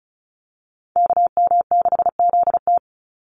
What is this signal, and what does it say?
Morse code, heard as a keyed tone, about 34 wpm, 700 Hz. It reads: KM6ZT
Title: Call Signs - 35wpm